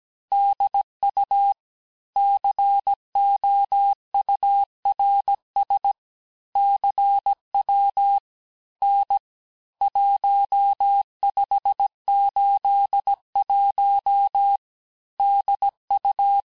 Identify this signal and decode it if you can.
Morse code, heard as a keyed tone, about 17 wpm, 800 Hz. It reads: DU COURS CW N 1581 DU
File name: cours-du-lundi files / corrige-cours-1581-texte-chiffre-vitesse-16-mpm